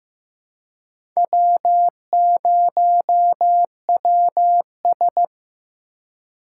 Morse code, heard as a keyed tone, about 18 wpm, 700 Hz. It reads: W0WS
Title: Speed Racing - USA 1x2 Call Signs 15wpm